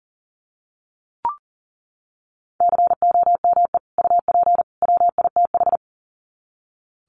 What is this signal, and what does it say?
Morse code, heard as a keyed tone, about 40 wpm, 700 Hz. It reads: COME UP WITH
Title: Sets of 3 Words 40wpm